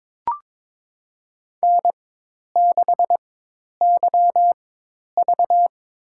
Morse code, heard as a keyed tone, about 23 wpm, spaced as for 10 wpm, 700 Hz. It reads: N6YV
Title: USA Call Signs - 4 Characters - 4x Character Spacing - 22wpm